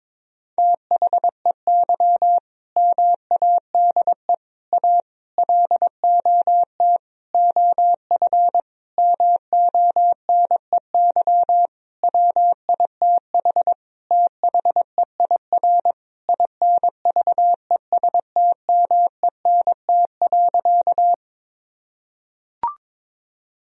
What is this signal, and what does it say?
Morse code, heard as a keyed tone, about 22 wpm, 700 Hz. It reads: THEY MADE A LOT OF MONEY WITH THEIR INVESTMENT.
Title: Sentences from Top 1000 Words - Review Part 4 - 22wpm